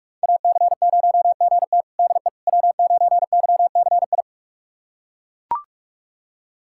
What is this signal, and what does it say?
Morse code, heard as a keyed tone, about 41 wpm, 700 Hz. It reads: AC0GT DE W9YCI
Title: USA Callsign DE Callsign - 5 Characters - 45wpm